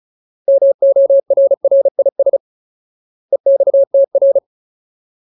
Morse code, heard as a keyed tone, about 34 wpm, 550 Hz. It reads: MORRIS EXTR
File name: Ultimate Word List / 35WPM Ultimate